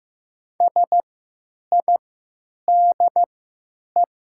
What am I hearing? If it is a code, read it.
Morse code, heard as a keyed tone, about 15 wpm, spaced as for 9 wpm, 700 Hz. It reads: SIDE
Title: 3x Character Spacing - Top 200 Words - 15wpm